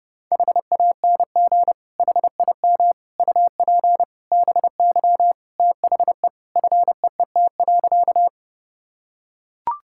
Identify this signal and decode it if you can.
Morse code, heard as a keyed tone, about 30 wpm, 700 Hz. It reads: HANG HIM UP BY THE FEET.
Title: Sentences from Top 1600 Words 30wpm